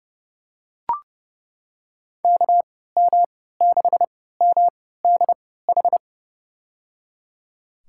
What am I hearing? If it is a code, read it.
Morse code, heard as a keyed tone, about 30 wpm, spaced as for 17 wpm, 700 Hz. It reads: KM6MDH
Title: USA Call Signs - 6 Characters - 3x Character Spacing - 30wpm